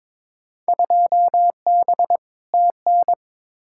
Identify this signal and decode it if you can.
Morse code, heard as a keyed tone, about 22 wpm, 700 Hz. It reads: 2B TN
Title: Field Day Exchanges Encore - 22wpm